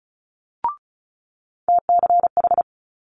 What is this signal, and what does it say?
Morse code, heard as a keyed tone, about 36 wpm, 700 Hz. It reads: TCH
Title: Multi-letter Phonemes Quick Fire 35wpm